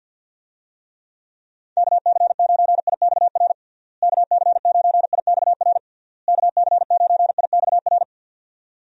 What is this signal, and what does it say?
Morse code, heard as a keyed tone, about 51 wpm, 700 Hz. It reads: KC9IXR KC9IXR KC9IXR
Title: USA Call Signs - 6 Characters - Repeated 3x - 50wpm